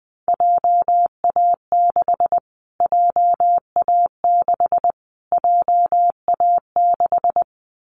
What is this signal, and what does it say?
Morse code, heard as a keyed tone, about 20 wpm, 700 Hz. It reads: JA6 JA6 JA6
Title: Callsign Prefixes for Top 25 DXCC - Repeated 3x - 20wpm